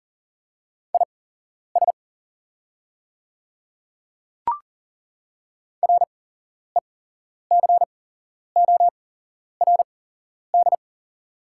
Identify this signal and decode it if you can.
Morse code, heard as a keyed tone, about 40 wpm, spaced as for 11 wpm, 690 Hz. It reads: IS RECORD
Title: 8x Character Spacing - Top 500 Words - 40wpm